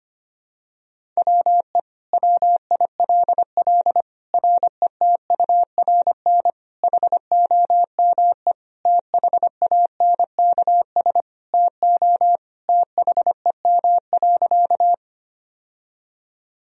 Morse code, heard as a keyed tone, about 25 wpm, 700 Hz. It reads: WE WILL RETURN HOME THANKS TO THEM.
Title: Sentences from Top 1000 Words - Review Part 3 - 25wpm